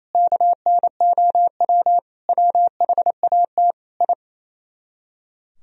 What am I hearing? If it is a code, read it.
Morse code, heard as a keyed tone, about 36 wpm, 700 Hz. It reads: KNOW WHAT I